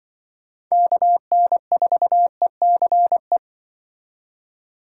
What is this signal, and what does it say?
Morse code, heard as a keyed tone, about 25 wpm, 700 Hz. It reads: KN4ECE
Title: Speed Racing - USA 2x3 Call Signs 20wpm